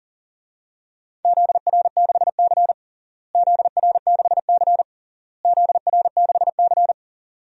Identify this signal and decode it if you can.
Morse code, heard as a keyed tone, about 37 wpm, 700 Hz. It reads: ZR6C ZR6C ZR6C
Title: DXCC-100 Call Signs - 4 Characters - Repeated 3x - 40wpm